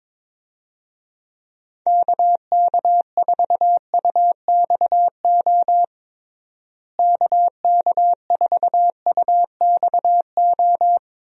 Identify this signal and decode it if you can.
Morse code, heard as a keyed tone, about 22 wpm, 700 Hz. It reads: KK4UXO KK4UXO
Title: USA Call Signs - 6 Characters - Repeated 3x - 22wpm